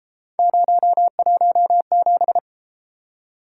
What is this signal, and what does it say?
Morse code, heard as a keyed tone, about 28 wpm, 700 Hz. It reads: 017